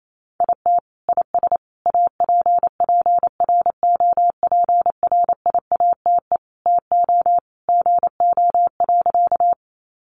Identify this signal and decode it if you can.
Morse code, heard as a keyed tone, about 28 wpm, 700 Hz. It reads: IT IS APPROPRIATE TO GO.
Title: Sentences from Top 900 Words 28wpm